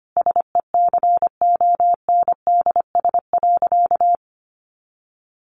Morse code, heard as a keyed tone, about 25 wpm, 700 Hz. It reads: SECONDS.
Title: A Sound Of Thunder 25wpm